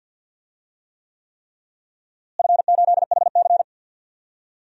Morse code, heard as a keyed tone, about 51 wpm, 700 Hz. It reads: F8HC
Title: DXCC-100 Call Signs - 4 Characters - Repeated 3x - 50wpm